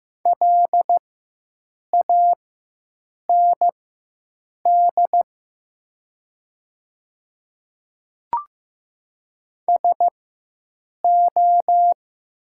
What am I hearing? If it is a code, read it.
Morse code, heard as a keyed tone, about 15 wpm, spaced as for 7 wpm, 700 Hz. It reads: LAND SO
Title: 4x Character Spacing - Top 200 Words - 15wpm